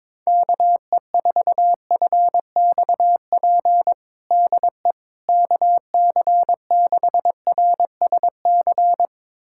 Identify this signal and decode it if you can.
Morse code, heard as a keyed tone, about 22 wpm, 700 Hz. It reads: KE4FXP DE KC6RSC
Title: USA Callsign DE Callsign - 6 Characters - 22wpm